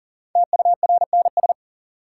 Morse code, heard as a keyed tone, about 40 wpm, 700 Hz. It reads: TURNS
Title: Sets of 5 Words Encore - New Word Drill - 40wpm